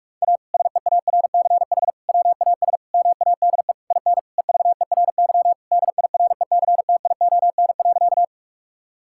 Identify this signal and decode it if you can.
Morse code, heard as a keyed tone, about 46 wpm, 700 Hz. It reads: A SEARCH WAS MADE IN EVERY DIRECTION.
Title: Sentences from Top 1000 Words - Review Part 5 - 45wpm